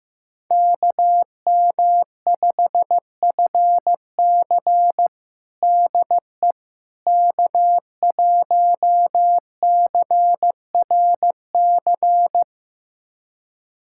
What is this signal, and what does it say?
Morse code, heard as a keyed tone, about 15 wpm, 700 Hz. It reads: KM5FC DE K1CRC
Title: USA Callsign DE Callsign - 5 Characters - 15wpm